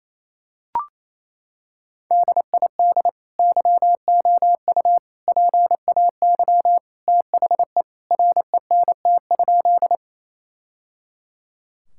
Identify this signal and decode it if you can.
Morse code, heard as a keyed tone, about 28 wpm, 700 Hz. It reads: DID YOU PAY THE RENT?
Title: Sentences from Top 1000 Words - Review Part 5 - 28wpm